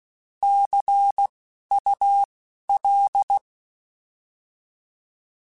Morse code, heard as a keyed tone, about 16 wpm, 800 Hz. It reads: C U L
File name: Abreviations / Abreviations mp3s / Abreviations@10